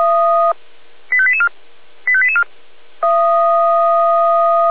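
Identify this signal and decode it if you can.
Five-tone selective call (ZVEI), heard as quick runs of five tones, tones of about 70 ms. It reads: 85884 85884
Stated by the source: Received on an SDR, ZVEI five-tone selective call.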